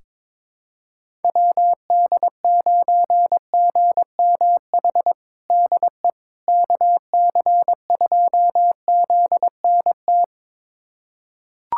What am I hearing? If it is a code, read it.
Morse code, heard as a keyed tone, about 22 wpm, 700 Hz. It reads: WD9GMH DE KC2ZNT
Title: USA Callsign DE Callsign 22wpm